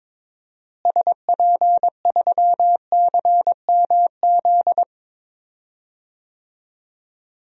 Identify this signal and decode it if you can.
Morse code, heard as a keyed tone, about 21 wpm, 700 Hz. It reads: SP3CMZ
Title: DXCC-100 Call Signs - 6 Characters - Repeated 2x - 22wpm